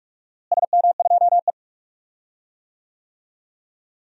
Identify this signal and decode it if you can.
Morse code, heard as a keyed tone, about 46 wpm, 700 Hz. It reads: SM2E
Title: DXCC-100 Call Signs - 4 Characters - Repeated 3x - 45wpm